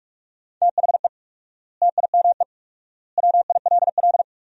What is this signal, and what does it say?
Morse code, heard as a keyed tone, about 46 wpm, 700 Hz. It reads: THE TIME WILL
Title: 4x Word-Spacing - Sentences from Top 100 Words - 45wpm